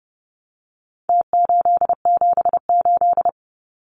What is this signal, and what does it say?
Morse code, heard as a keyed tone, about 30 wpm, 700 Hz. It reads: T878
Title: Numbers - Contest Style - 4-Digit 30wpm